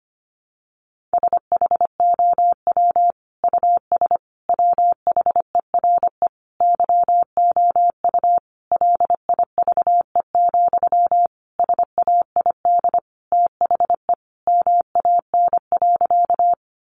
Morse code, heard as a keyed tone, about 25 wpm, 700 Hz. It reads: SHOW US WHERE YOU LIVE, SAID THE MAN.
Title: Sentences from Top 200 Words 25wpm